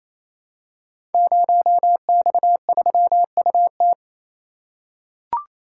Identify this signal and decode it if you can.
Morse code, heard as a keyed tone, about 27 wpm, 700 Hz. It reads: 0X3UT
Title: Code Groups - 5 Letter:Numbers - 28wpm